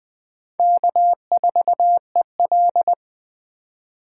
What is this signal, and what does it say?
Morse code, heard as a keyed tone, about 20 wpm, 700 Hz. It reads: K4EL
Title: Speed Racing - USA 1x2 Call Signs 17wpm